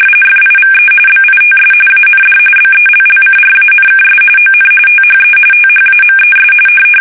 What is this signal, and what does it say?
RTTY, about 45.45 baud. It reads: FIVE BITS AND BELLCRANKS ON THE THAMES
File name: fsk850 / battleships